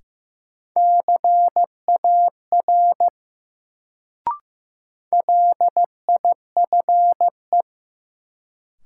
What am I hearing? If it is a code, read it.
Morse code, heard as a keyed tone, about 15 wpm, 700 Hz. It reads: CAR LIFE
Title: Sentences from Top 300 Words - New Word Drill - 15wpm